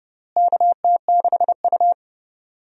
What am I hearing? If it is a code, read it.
Morse code, heard as a keyed tone, about 30 wpm, 700 Hz. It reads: KT6U
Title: USA Call Signs - 4 Characters - Repeated 2x - 30wpm